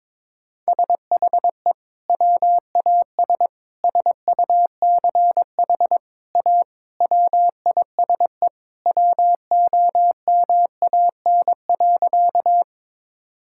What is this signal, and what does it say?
Morse code, heard as a keyed tone, about 22 wpm, 700 Hz. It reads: SHE WAS SUCH A WISE WOMAN.
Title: Sentences from Top 1600 Words 22wpm